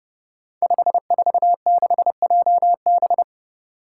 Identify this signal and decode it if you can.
Morse code, heard as a keyed tone, about 30 wpm, 700 Hz. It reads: 546JB
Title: Five Challenging Letter-Number Combinations 30wpm